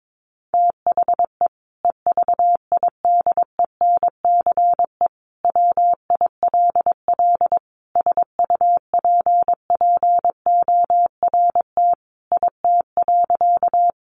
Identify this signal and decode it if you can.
Morse code, heard as a keyed tone, about 22 wpm, 700 Hz. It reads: THE EVIDENCE WILL SUPPORT IT.
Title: Sentences from Top 1300 Words 22wpm